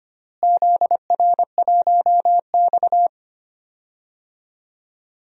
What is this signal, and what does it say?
Morse code, heard as a keyed tone, about 25 wpm, 700 Hz. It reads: ZR1X
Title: DXCC-100 Call Signs - 4 Characters - Repeated 2x - 25wpm